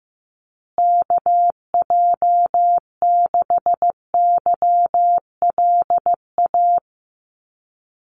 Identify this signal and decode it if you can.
Morse code, heard as a keyed tone, about 15 wpm, 700 Hz. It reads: KJ6YLA
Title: USA Call Signs Repeated 2x 15wpm